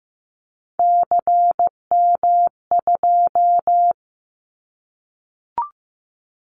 Morse code, heard as a keyed tone, about 15 wpm, 700 Hz. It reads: CM2
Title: Callsign Prefixes for Top 100 DXCC - 15wpm